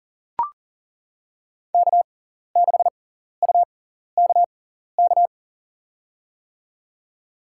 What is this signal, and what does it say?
Morse code, heard as a keyed tone, about 40 wpm, spaced as for 14 wpm, 700 Hz. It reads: K6UKK
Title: USA Call Signs - 6x Character Spacing - 40wpm